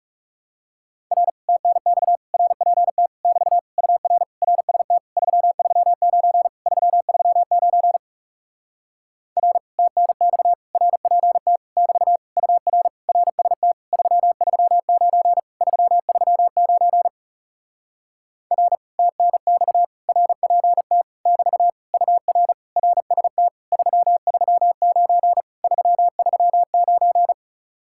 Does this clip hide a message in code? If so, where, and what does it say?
Yes: Morse code from 1.1 to 27.3 s, heard as a keyed tone, about 40 wpm, 700 Hz. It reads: R TNX RPT <BT> UR RST 339 339 R TNX RPT <BT> UR RST 339 339 R TNX RPT <BT> UR RST 339 339